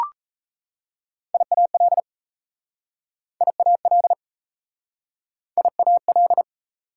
Morse code, heard as a keyed tone, about 37 wpm, 700 Hz. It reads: IAL IAL IAL